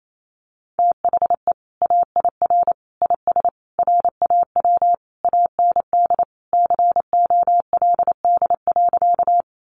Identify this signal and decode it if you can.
Morse code, heard as a keyed tone, about 28 wpm, 700 Hz. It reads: THE AIR IS RAW AND COLD.
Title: Sentences from Top 1500 Words 28wpm